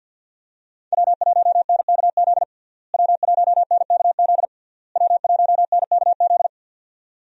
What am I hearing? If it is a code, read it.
Morse code, heard as a keyed tone, about 49 wpm, 700 Hz. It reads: W1NKZ W1NKZ W1NKZ
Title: USA Call Signs - 5 Characters - Repeated 3x - 50wpm